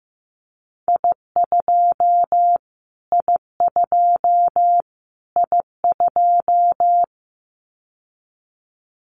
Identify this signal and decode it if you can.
Morse code, heard as a keyed tone, about 15 wpm, 700 Hz. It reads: I2 I2 I2
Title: Callsign Prefixes for Top 25 DXCC - Repeated 3x - 15wpm